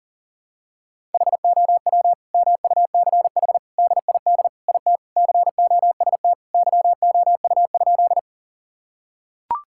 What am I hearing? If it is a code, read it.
Morse code, heard as a keyed tone, about 40 wpm, 700 Hz. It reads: HOW MUCH DID IT COST YOU?
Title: Sentences from Top 1000 Words - Review Part 5 - 40wpm